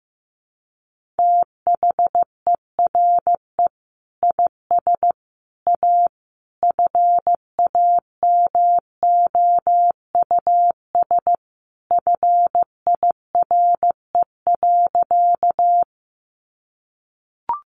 Morse code, heard as a keyed tone, about 15 wpm, 700 Hz. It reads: THERE IS A FAMOUS FIRE.
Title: Sentences from Top 1500 Words 15wpm